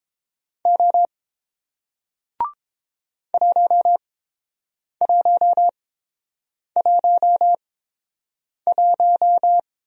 Morse code, heard as a keyed tone, about 30 wpm, spaced as for 7 wpm, 700 Hz. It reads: O 1111